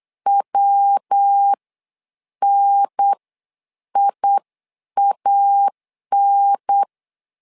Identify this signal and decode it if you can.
Morse code, heard as a keyed tone, about 8 wpm, spaced as for 7 wpm, 800 Hz. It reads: WNIAN
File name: LEZIONI / 3 MWRO / 3 MW 8wpm 2501